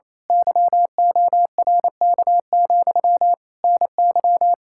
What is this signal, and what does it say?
Morse code, heard as a keyed tone, about 28 wpm, 700 Hz. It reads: YORK, NY
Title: Speed-Racing Common QSO Phrases Encore - 28wpm